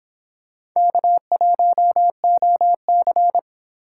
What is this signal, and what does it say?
Morse code, heard as a keyed tone, about 25 wpm, 700 Hz. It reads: K1OC